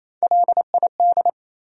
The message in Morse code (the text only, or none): LID